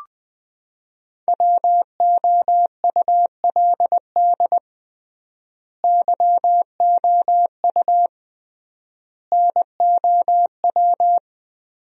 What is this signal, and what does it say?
Morse code, heard as a keyed tone, about 20 wpm, 700 Hz. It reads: WOULD YOU NOW
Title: 3x Word-Spacing - Sentences from Top 200 Words - 20wpm